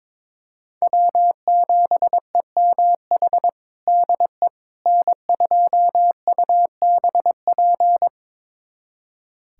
Morse code, heard as a keyed tone, about 22 wpm, 700 Hz. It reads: W7EMH DE N2UBP
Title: USA Callsign DE Callsign - 5 Characters - 22wpm